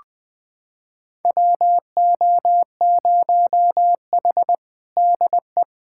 Morse code, heard as a keyed tone, about 20 wpm, 700 Hz. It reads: WO0H DE
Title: USA Callsign DE Callsign - 4 Characters - 20wpm